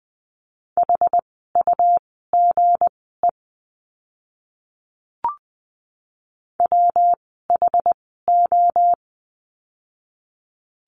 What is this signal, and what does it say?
Morse code, heard as a keyed tone, about 20 wpm, spaced as for 15 wpm, 700 Hz. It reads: HUGE WHO
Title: 160 Lesson 26f - Letter g - Words 20wpm